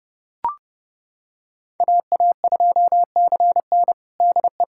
Morse code, heard as a keyed tone, about 32 wpm, 700 Hz. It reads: AA2CN DE
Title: USA Callsign DE Callsign - 5 Characters - 30wpm